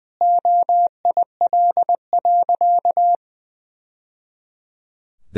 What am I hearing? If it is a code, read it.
Morse code, heard as a keyed tone, about 20 wpm, 700 Hz. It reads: OIL.
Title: Sentences from Top 1000 Words - Review Part 4 - 20wpm